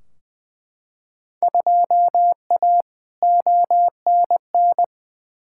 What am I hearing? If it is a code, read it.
Morse code, heard as a keyed tone, about 20 wpm, 700 Hz. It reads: 2A ONN
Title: Field Day Exchanges Encore - 20wpm